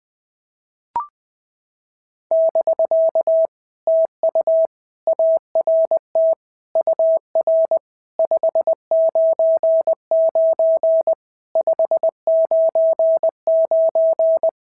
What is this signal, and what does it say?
Morse code, heard as a keyed tone, about 20 wpm, 650 Hz. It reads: <BK> TU ART UR 599 599
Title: POTA QSO Practice - 20wpm